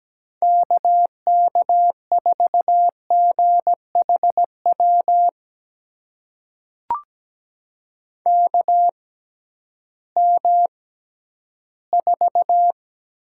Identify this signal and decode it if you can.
Morse code, heard as a keyed tone, about 17 wpm, spaced as for 11 wpm, 700 Hz. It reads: KK4GHW KM4
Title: USA Call Signs - 6 Characters - 6x Character Spacing - 17wpm